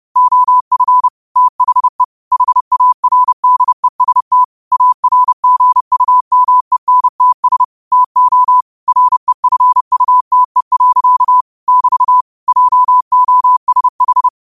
Morse code, heard as a keyed tone, about 30 wpm, 1000 Hz. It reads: OF THE HARDEST ARGUMENTS TO REFUTE. <BT> JOSH
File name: qotdfiles / 30wpm